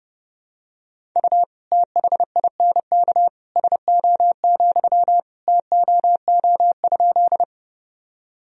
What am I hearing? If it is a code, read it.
Morse code, heard as a keyed tone, about 32 wpm, 700 Hz. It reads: U THINK SO, TOO?
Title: Sentences from Top 200 Words and Abbreviations 30wpm